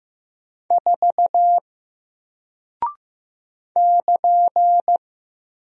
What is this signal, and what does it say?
Morse code, heard as a keyed tone, about 15 wpm, 700 Hz. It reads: 4 <KN>
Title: Single Character (Letter:Number:Punctuation:Prosign) Alt 15wpm